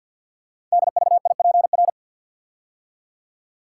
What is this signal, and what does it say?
Morse code, heard as a keyed tone, about 51 wpm, 700 Hz. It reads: DVIPR
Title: Code Groups - 5 Letters - 50wpm